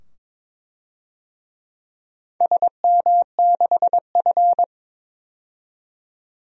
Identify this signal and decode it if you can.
Morse code, heard as a keyed tone, about 22 wpm, 700 Hz. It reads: SM6F
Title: DXCC-100 Call Signs - 4 Characters - 22wpm